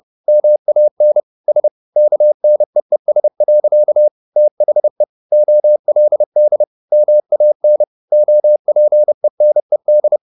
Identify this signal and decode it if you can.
Morse code, heard as a keyed tone, about 30 wpm, 600 Hz. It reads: MAN S KNEES. THE OLD MAN OPENED
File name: the old man and the sea / the old man and the sea 1 30wpm